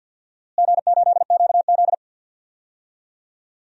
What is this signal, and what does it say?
Morse code, heard as a keyed tone, about 50 wpm, 700 Hz. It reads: G8QZ
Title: DXCC-100 Call Signs - 4 Characters - Repeated 2x - 50wpm